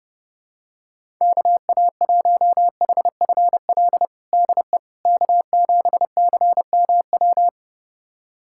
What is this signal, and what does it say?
Morse code, heard as a keyed tone, about 32 wpm, 700 Hz. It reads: KA1HFL DE K7CMW
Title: USA Callsign DE Callsign 30wpm